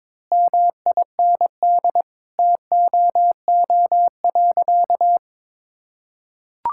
Morse code, heard as a keyed tone, about 22 wpm, 700 Hz. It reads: MIND TOO.
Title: Sentences from Top 1000 Words - Review Part 6 - 22wpm